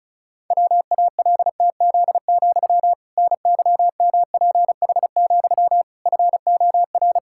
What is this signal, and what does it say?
Morse code, heard as a keyed tone, about 36 wpm, 700 Hz. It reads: WALTZ, NYMPH, FOR
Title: Pangram Challenge v2 - 35wpm